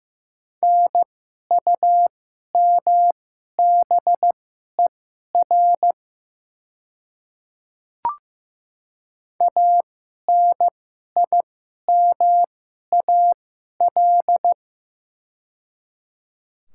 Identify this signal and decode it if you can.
Morse code, heard as a keyed tone, about 15 wpm, spaced as for 11 wpm, 690 Hz. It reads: NUMBER ANIMAL
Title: 2x Character Spacing - Top 300 Words - 15wpm